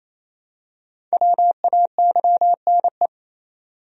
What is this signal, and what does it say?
Morse code, heard as a keyed tone, about 28 wpm, 700 Hz. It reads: WAYNE
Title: USA Common Names - Max 5 Letters - 28wpm